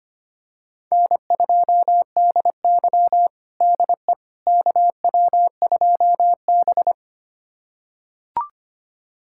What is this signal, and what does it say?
Morse code, heard as a keyed tone, about 25 wpm, 700 Hz. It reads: N2DY DE KW2B
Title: USA Callsign DE Callsign - 4 Characters - 25wpm